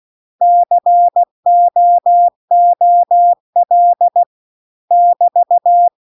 Morse code, heard as a keyed tone, about 16 wpm, 700 Hz. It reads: COOL <BT>
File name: CW-The-Big-Twoheaded-River-2-16WPM